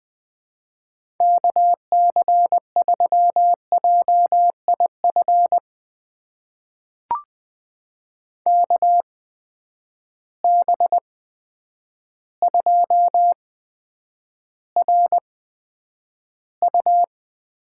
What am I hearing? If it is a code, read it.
Morse code, heard as a keyed tone, about 20 wpm, spaced as for 9 wpm, 700 Hz. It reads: KC3JIF KB2RU
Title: USA Call Signs - 6 Characters - 8x Character Spacing - 20wpm